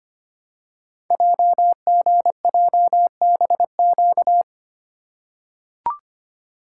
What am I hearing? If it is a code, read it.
Morse code, heard as a keyed tone, about 25 wpm, 700 Hz. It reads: JGJBQ
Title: Code Groups - 5 Letters - 25wpm